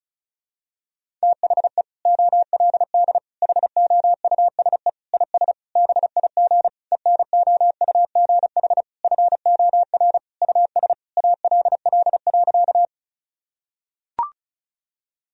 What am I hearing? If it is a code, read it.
Morse code, heard as a keyed tone, about 36 wpm, 700 Hz. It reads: THE OLD HOUSE IS BIG ENOUGH FOR US ALL.
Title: Sentences from Top 1000 Words - Review Part 1 - 35wpm